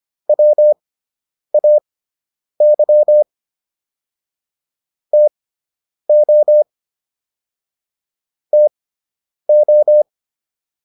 Morse code, heard as a keyed tone, about 25 wpm, spaced as for 9 wpm, 600 Hz. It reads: WAY TO TO